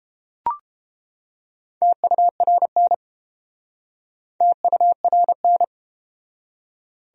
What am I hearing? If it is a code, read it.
Morse code, heard as a keyed tone, about 29 wpm, 700 Hz. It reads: TURN TURN